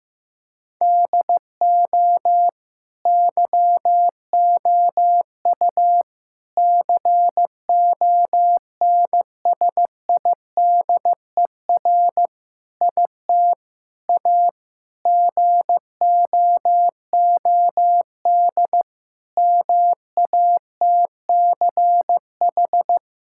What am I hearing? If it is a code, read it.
Morse code, heard as a keyed tone, about 16 wpm, 700 Hz. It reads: DO YOU CONSIDER IT A GOOD MATCH
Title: Sentences from Top 1600 Words 15wpm